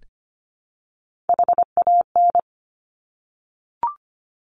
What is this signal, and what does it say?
Morse code, heard as a keyed tone, about 25 wpm, 700 Hz. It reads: HAN
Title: Top Three Letter Combinations in Words 25wpm